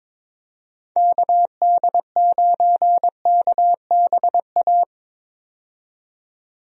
Morse code, heard as a keyed tone, about 22 wpm, 700 Hz. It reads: KD9KBA
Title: USA Call Signs - 6 Characters - 22wpm